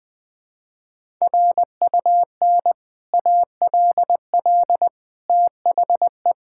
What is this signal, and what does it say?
Morse code, heard as a keyed tone, about 20 wpm, 700 Hz. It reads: RUN ALL THE
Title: Sentences with 3-Word Phrases - 20wpm